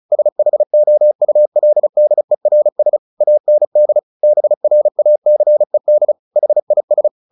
Morse code, heard as a keyed tone, about 34 wpm, 600 Hz. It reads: SHOULDERS AND BRACED HIS
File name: the old man and the sea 2 35wpm